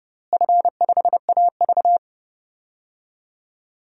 Morse code, heard as a keyed tone, about 32 wpm, 700 Hz. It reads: F5AV
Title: DXCC-100 Call Signs - 4 Characters - Repeated 2x - 30wpm